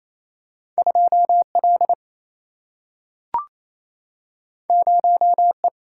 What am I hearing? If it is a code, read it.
Morse code, heard as a keyed tone, about 28 wpm, 700 Hz. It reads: 2 L 0 E
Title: Single-Letter Single-Number 28wpm